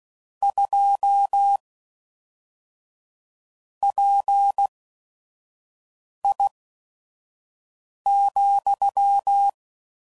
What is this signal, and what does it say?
Morse code, heard as a keyed tone, about 16 wpm, spaced as for 5 wpm, 800 Hz. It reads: 2 PI,
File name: CW Code Groups @5WPM / Technical Subjects / XL